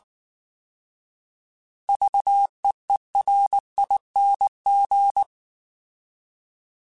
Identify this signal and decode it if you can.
Morse code, heard as a keyed tone, about 19 wpm, 800 Hz. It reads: VEERING